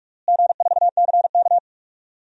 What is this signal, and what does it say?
Morse code, heard as a keyed tone, about 46 wpm, 700 Hz. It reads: G4CK
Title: DXCC-100 Call Signs - 4 Characters - Repeated 3x - 45wpm